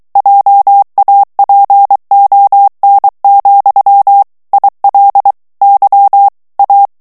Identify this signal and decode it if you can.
Morse code, heard as a keyed tone, about 23 wpm, 800 Hz. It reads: JAPON, IL Y A
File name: FRANCAIS TEXTE CLAIR 1 VITESSE 22 MOTS MINUTES